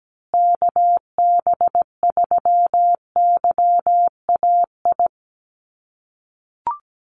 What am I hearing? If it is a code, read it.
Morse code, heard as a keyed tone, about 17 wpm, 700 Hz. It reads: KB3YAI
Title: USA Call Signs Repeated 2x 17wpm